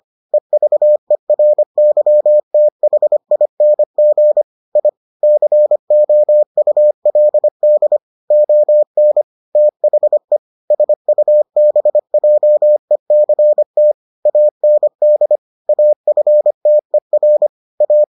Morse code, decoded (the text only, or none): EVERYTHING I COULD ON THE SUBJECT AND AFTER A